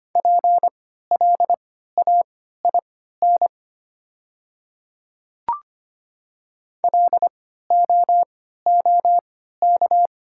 Morse code, heard as a keyed tone, about 25 wpm, spaced as for 14 wpm, 700 Hz. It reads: PLAIN LOOK
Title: 3x Character Spacing - Top 500 Words - 25wpm